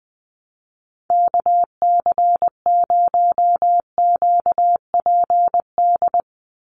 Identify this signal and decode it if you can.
Morse code, heard as a keyed tone, about 20 wpm, 700 Hz. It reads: KC0QPD
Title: USA Call Signs 20wpm